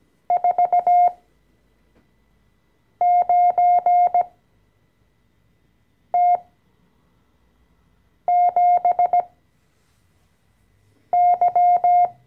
Morse code, heard as a keyed tone, about 17 wpm, spaced as for 4 wpm, 710 Hz. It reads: 49T7Y
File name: CW-Online-Kurs / Anfänger / Lektion 014